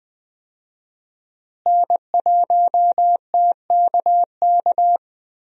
Morse code, heard as a keyed tone, about 20 wpm, 700 Hz. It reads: N1TKK
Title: USA Call Signs - 5 Characters - Repeated 3x - 20wpm